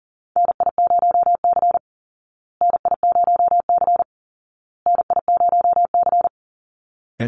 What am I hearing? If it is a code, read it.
Morse code, heard as a keyed tone, about 40 wpm, 700 Hz. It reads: NI0C NI0C NI0C
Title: Call Signs Repeated 2x - 40wpm